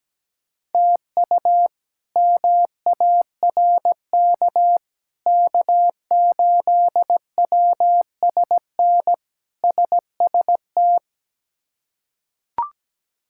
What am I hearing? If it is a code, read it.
Morse code, heard as a keyed tone, about 17 wpm, 700 Hz. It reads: TU MARK K8WSN SST
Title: K1USN SST QSO Practice - 17wpm